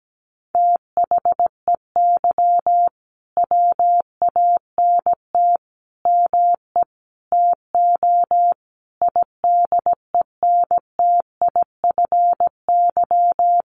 Morse code, heard as a keyed tone, about 17 wpm, 700 Hz. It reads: THEY WANT ME TO IDENTIFY
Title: Sentences from Top 900 Words 17wpm